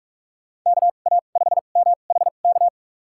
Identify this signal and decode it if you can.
Morse code, heard as a keyed tone, about 41 wpm, spaced as for 31 wpm, 700 Hz. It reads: KA5MHK
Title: USA Call Signs - 6 Characters - 2x Character Spacing - 45wpm